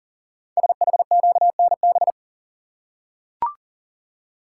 Morse code, heard as a keyed tone, about 40 wpm, 700 Hz. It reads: SHQNB
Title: Code Groups - 5 Letters - 40wpm